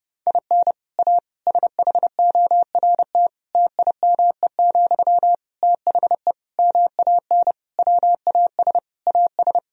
Morse code, heard as a keyed tone, about 32 wpm, 700 Hz. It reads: IN A SHORT TIME, THE MAN WAS AS
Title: Sentences from Top 1000 Words - Review Part 2 - 30wpm